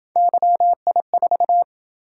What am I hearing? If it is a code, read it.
Morse code, heard as a keyed tone, about 27 wpm, 700 Hz. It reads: YI4